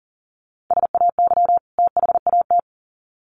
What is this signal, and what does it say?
Morse code, heard as a keyed tone, about 40 wpm, 700 Hz. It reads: SAY THAT
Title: Sets of 2 Words 40wpm